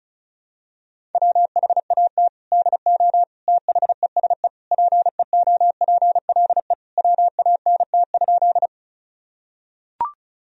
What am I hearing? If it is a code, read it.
Morse code, heard as a keyed tone, about 36 wpm, 700 Hz. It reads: WHAT DO THESE PEOPLE WANT?
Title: Sentences from Top 1000 Words - Review Part 1 - 35wpm